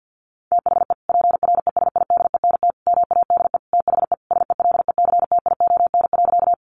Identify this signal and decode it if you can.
Morse code, heard as a keyed tone, about 48 wpm, 700 Hz. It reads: THE PRESIDENT MADE THE SELECTION.
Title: Sentences from Top 1400 Words 50wpm